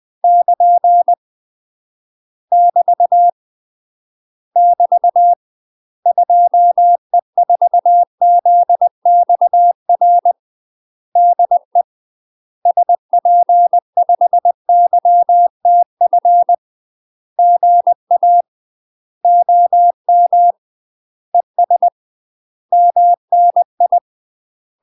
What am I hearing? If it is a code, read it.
Morse code, heard as a keyed tone, about 20 wpm, 700 Hz. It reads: <KN> <BT> <BT> 2E4ZXR DE SP5YTF GA OM ES MNI
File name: UK MASTER QSOs TRAINING / 20WPM / UK MASTER QSOs 0009